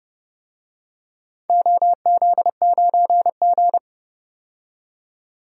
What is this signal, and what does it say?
Morse code, heard as a keyed tone, about 30 wpm, 700 Hz. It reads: OZ9G
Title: DXCC-100 Call Signs - 4 Characters - 30wpm